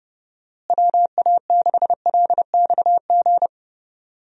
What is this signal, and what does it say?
Morse code, heard as a keyed tone, about 31 wpm, 700 Hz. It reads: WA6LXG
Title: USA Call Signs - 6 Characters - Repeated 3x - 30wpm